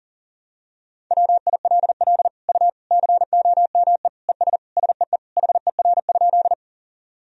Morse code, heard as a keyed tone, about 40 wpm, 700 Hz. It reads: WILL U COME ES SEE HER?
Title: Sentences from Top 100 Words and Abbreviations 40wpm